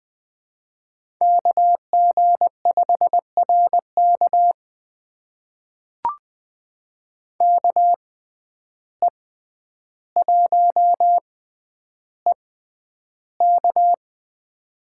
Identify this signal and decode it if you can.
Morse code, heard as a keyed tone, about 20 wpm, spaced as for 10 wpm, 700 Hz. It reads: KG5RK KE1EK
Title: USA Call Signs - 5 Characters - 6x Character Spacing - 20wpm